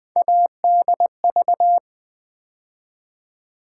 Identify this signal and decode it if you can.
Morse code, heard as a keyed tone, about 21 wpm, 700 Hz. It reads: ADV
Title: Build A Word - 3 Letters At A Time - 8-Letter Words - 20wpm